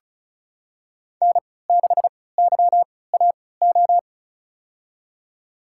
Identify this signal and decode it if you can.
Morse code, heard as a keyed tone, about 34 wpm, spaced as for 20 wpm, 700 Hz. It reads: N6YAO
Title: USA Call Signs - 3x Character Spacing - 35wpm